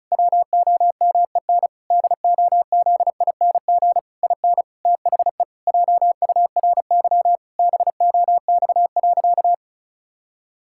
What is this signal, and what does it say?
Morse code, heard as a keyed tone, about 36 wpm, 700 Hz. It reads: WOMEN DOZING IN THE JURY BOX.
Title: Pangram Challenge v2 - 35wpm